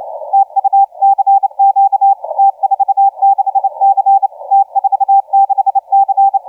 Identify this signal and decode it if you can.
Morse code, heard as a keyed tone, about 30 wpm, 790 Hz, with radio noise. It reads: TU CQ T46C T46C
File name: T46C 0435UTC 20m